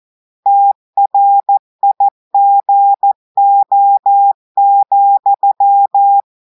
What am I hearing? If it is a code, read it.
Morse code, heard as a keyed tone, about 14 wpm, 800 Hz. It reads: TRIGO,
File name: audios / portugais / portugais texte en clair 1 vitesse 14 mots minutes